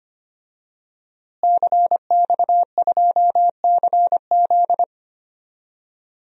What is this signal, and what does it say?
Morse code, heard as a keyed tone, about 25 wpm, 700 Hz. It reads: CX2CZ
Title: DXCC-100 Call Signs - 5 Characters - Repeated 3x - 25wpm